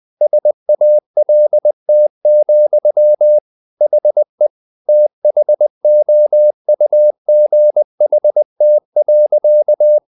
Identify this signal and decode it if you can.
Morse code, heard as a keyed tone, about 20 wpm, 600 Hz. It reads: SALT, HE THOUGHT.
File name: the old man and the sea part 2 20wpm